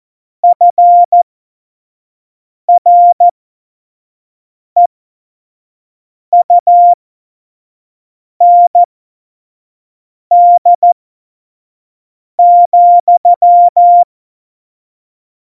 Morse code, heard as a keyed tone, about 14 wpm, spaced as for 5 wpm, 700 Hz. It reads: FREUND,
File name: Morsekurs / Witze / Witz 29. September 2020, 25 F70